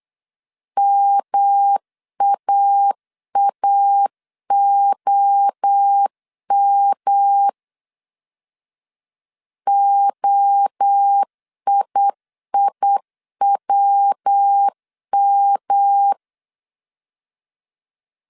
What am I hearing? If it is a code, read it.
Morse code, heard as a keyed tone, about 8 wpm, 800 Hz. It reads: MAAOM OIIWM